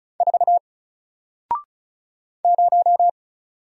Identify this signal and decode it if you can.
Morse code, heard as a keyed tone, about 36 wpm, 700 Hz. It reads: 4 0
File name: Single-Letter-Number-35wpm